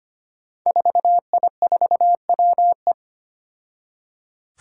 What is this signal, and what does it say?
Morse code, heard as a keyed tone, about 25 wpm, 700 Hz. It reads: 4I4WE
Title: Five Challenging Letter-Number Combinations 25wpm